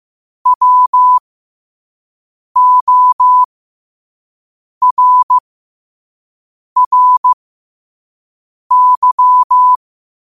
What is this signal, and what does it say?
Morse code, heard as a keyed tone, about 15 wpm, spaced as for 5 wpm, 1000 Hz. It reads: WORRY
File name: qotdfiles / 5wpm